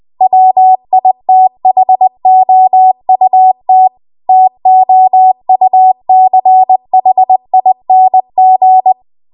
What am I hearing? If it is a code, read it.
Morse code, heard as a keyed tone, about 20 wpm, 750 Hz. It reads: WITHOUT TOUCHING
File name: Archive / 20 WPM / 20WPM